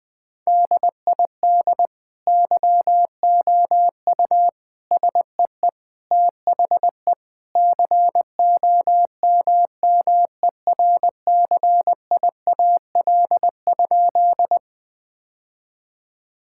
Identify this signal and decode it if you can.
Morse code, heard as a keyed tone, about 21 wpm, 700 Hz. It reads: DID YOU SEE THE COMMERCIAL?
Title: Sentences from Top 1000 Words - Review Part 5 - 20wpm